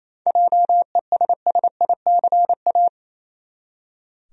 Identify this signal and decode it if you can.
Morse code, heard as a keyed tone, about 28 wpm, 700 Hz. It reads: JESSICA
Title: USA Common Names - 28wpm